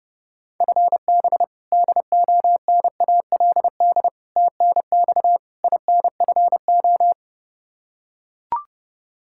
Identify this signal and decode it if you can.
Morse code, heard as a keyed tone, about 30 wpm, 700 Hz. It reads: FB DONALD TNX INFO
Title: Practice QSO - Exchange 3 - More Info - 30wpm